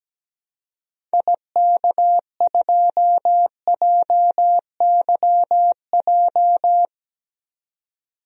Morse code, heard as a keyed tone, about 17 wpm, 700 Hz. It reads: IK2JYJ
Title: DXCC-100 Call Signs - 6 Characters - 17wpm